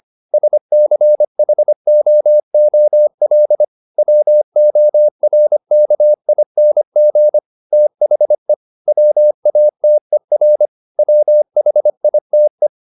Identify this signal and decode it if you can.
Morse code, heard as a keyed tone, about 25 wpm, 600 Hz. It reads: SCHOOL WORKING THE WATER WHITE
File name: the old man and the sea part 1 25wpm